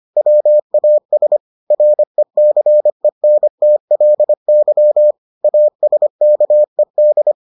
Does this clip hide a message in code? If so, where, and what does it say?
Yes: Morse code from 0.2 to 7.3 s, heard as a keyed tone, about 25 wpm, 600 Hz. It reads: WAS RECENTLY ASKED